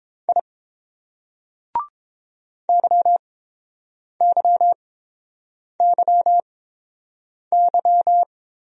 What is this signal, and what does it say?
Morse code, heard as a keyed tone, about 30 wpm, spaced as for 7 wpm, 700 Hz. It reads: I YYYY